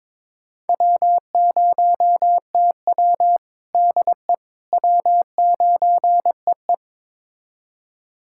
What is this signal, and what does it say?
Morse code, heard as a keyed tone, about 22 wpm, 700 Hz. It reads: W0TW DE W9EE
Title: USA Callsign DE Callsign - 4 Characters - 22wpm